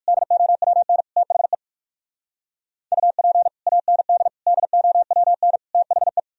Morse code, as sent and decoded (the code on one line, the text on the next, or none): -.. --- .-- -. / - .... / ..- .--. / .- -. -.. / -.. --- .-- -. / - .... .
DOWN THE UP AND DOWN THE